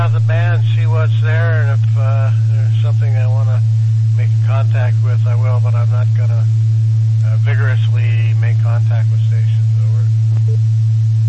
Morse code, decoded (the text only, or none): none